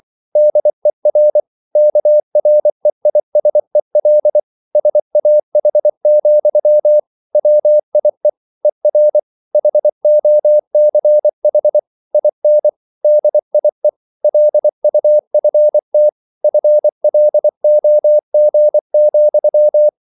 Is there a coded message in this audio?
Yes: Morse code from 0.4 to 19.9 s, heard as a keyed tone, about 24 wpm, 600 Hz. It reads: DER KREISEL SAH, WIE ER HOCH IN DIE LUFT FLOG,